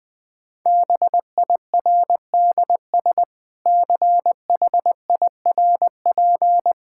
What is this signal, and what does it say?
Morse code, heard as a keyed tone, about 20 wpm, 700 Hz. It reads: BIRDS CHIRP
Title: Difficult Sentences with Hints - 20wpm